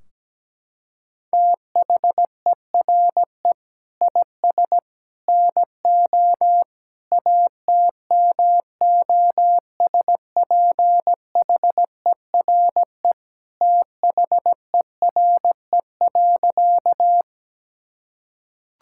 Morse code, heard as a keyed tone, about 17 wpm, 700 Hz. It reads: THERE IS NO ATMOSPHERE THERE.
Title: Sentences from Top 1600 Words 17wpm